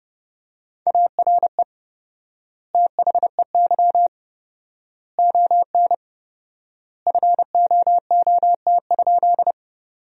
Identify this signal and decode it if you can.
Morse code, heard as a keyed tone, about 31 wpm, 700 Hz. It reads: ARE THEY ON FOOT?
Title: 4x Word-Spacing - Sentences from Top 500 Words - 30wpm